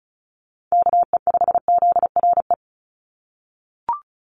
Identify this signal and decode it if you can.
Morse code, heard as a keyed tone, about 34 wpm, 700 Hz. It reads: KE5ZRE
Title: USA Call Signs 35wpm